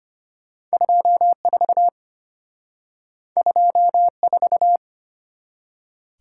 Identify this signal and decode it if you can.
Morse code, heard as a keyed tone, about 32 wpm, 700 Hz. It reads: 24 24